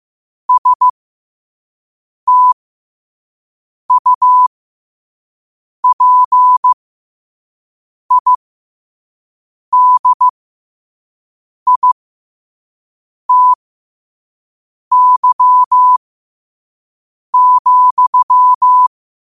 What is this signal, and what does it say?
Morse code, heard as a keyed tone, about 15 wpm, spaced as for 5 wpm, 1000 Hz. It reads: STUPIDITY,
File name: qotdfiles / 5wpm